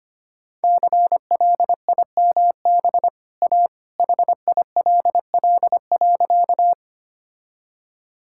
Morse code, heard as a keyed tone, about 25 wpm, 690 Hz. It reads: CLIMB A HILL.
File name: Easy Sentences with Hints - 25wpm